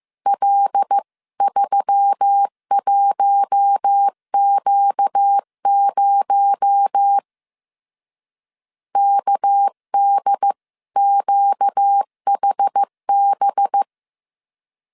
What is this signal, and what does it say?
Morse code, heard as a keyed tone, about 15 wpm, 800 Hz. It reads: L31Q0 KDQHB